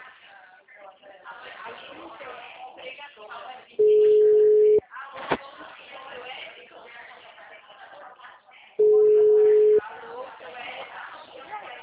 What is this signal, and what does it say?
Ringback tone (Europe, 425 Hz), with speech.
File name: out- -2010- - - .5899